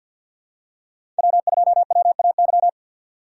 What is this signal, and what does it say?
Morse code, heard as a keyed tone, about 52 wpm, 700 Hz. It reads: W2WAY
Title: USA Call Signs - 5 Characters - 2x Character Spacing - 50wpm